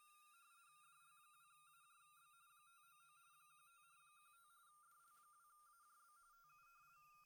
RTTY, about 45.45 baud. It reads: DL2AJB 599 15 15 DL2AJB TU
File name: winrad sdrx 7053KHZ